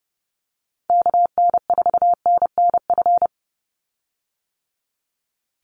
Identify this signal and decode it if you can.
Morse code, heard as a keyed tone, about 30 wpm, 700 Hz. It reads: KN4NNF
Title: USA Call Signs 30wpm